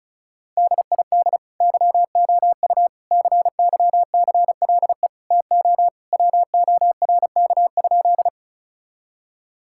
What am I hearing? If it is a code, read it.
Morse code, heard as a keyed tone, about 36 wpm, 700 Hz. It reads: DID YOU CYCLE TO WORK?
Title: Sentences from Top 1000 Words - Review Part 6 - 35wpm